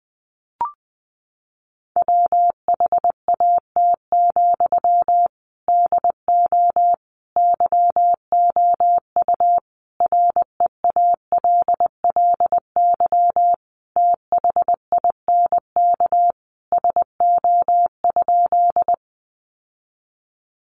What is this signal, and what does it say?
Morse code, heard as a keyed tone, about 20 wpm, 700 Hz. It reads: WHAT, DO YOU REALLY THINK SO?
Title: Sentences from Top 600 Words 20wpm